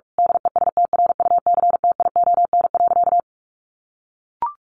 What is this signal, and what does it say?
Morse code, heard as a keyed tone, about 44 wpm, 700 Hz. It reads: DESTRUCTION.
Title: A Sound Of Thunder 45wpm